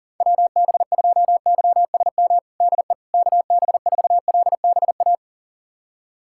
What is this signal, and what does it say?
Morse code, heard as a keyed tone, about 37 wpm, 700 Hz. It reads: WB2YSM DE KB4LBA
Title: USA Callsign DE Callsign 40wpm